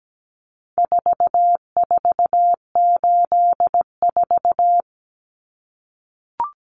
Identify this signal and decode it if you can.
Morse code, heard as a keyed tone, about 17 wpm, 700 Hz. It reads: 4484
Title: Common Cut Numbers - 4-Digit 17wpm